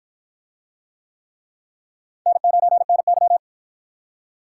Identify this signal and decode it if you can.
Morse code, heard as a keyed tone, about 53 wpm, 700 Hz. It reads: N9NY